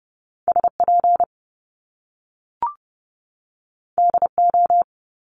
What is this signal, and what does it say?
Morse code, heard as a keyed tone, about 30 wpm, 700 Hz. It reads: SP DO
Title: Top Two Letter Combinations in Words 30wpm